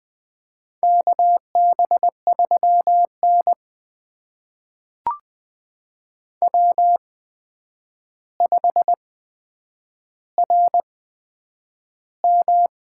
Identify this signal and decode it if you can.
Morse code, heard as a keyed tone, about 20 wpm, spaced as for 9 wpm, 700 Hz. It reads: KB3N W5RM
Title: USA Call Signs - 4 Characters - 8x Character Spacing - 20wpm